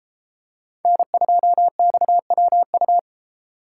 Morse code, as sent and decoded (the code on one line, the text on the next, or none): -. ..--- -..- .-- ..-
N2XWU